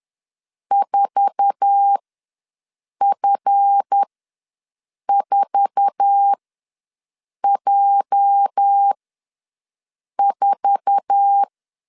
Morse code, heard as a keyed tone, about 10 wpm, spaced as for 6 wpm, 800 Hz. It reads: 4F4J4
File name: LEZIONI / 8 YJ45 / 8 YJ4 ws 10wpm 4020